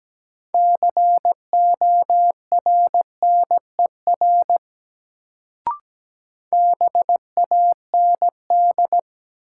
Morse code, heard as a keyed tone, about 17 wpm, 700 Hz. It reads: CORNER BAND
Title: Sentences from Top 1600 Words - New Word Drill Alt - 17wpm